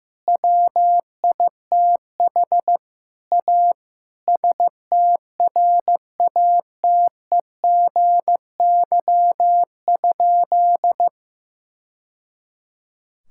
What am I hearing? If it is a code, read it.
Morse code, heard as a keyed tone, about 15 wpm, 700 Hz. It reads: WITH A STRATEGY?
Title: Sentences from Top 1000 Words - Review Part 5 - 15wpm